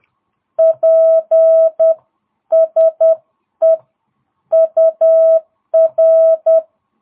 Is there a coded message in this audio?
Yes: Morse code from 0.6 to 6.6 s, heard as a keyed tone, about 10 wpm, 660 Hz. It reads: PSE UR